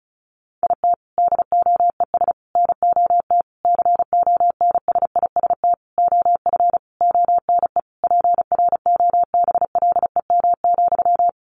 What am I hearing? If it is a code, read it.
Morse code, heard as a keyed tone, about 34 wpm, 700 Hz. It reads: IT DOES NOT CONSIST OF ONE PROBLEM,
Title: Sentences from Top 1400 Words 35wpm